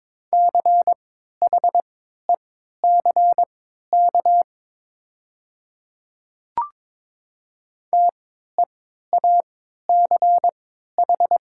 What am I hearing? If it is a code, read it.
Morse code, heard as a keyed tone, about 22 wpm, spaced as for 13 wpm, 700 Hz. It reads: CHECK TEACH
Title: 3x Character Spacing - Top 500 Words - 22wpm